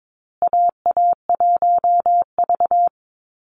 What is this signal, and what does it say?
Morse code, heard as a keyed tone, about 22 wpm, 700 Hz. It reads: AA1V
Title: Call Signs Repeated 2x - 22wpm